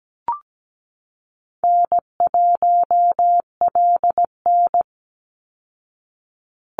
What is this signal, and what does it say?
Morse code, heard as a keyed tone, about 17 wpm, 700 Hz. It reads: N1LN
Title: Call Signs Repeated 2x - 17wpm